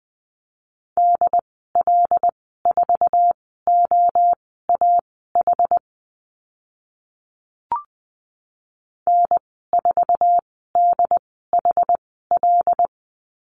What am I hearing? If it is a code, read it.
Morse code, heard as a keyed tone, about 20 wpm, spaced as for 14 wpm, 700 Hz. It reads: DL4OAH N4DHL
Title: 076 Lesson 13g - Letter L - Call Signs 20wpm